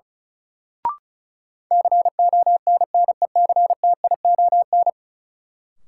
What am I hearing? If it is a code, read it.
Morse code, heard as a keyed tone, about 34 wpm, 700 Hz. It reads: CONNECTION
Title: Sentences from Top 1300 Words - New Word Drill Alt - 35wpm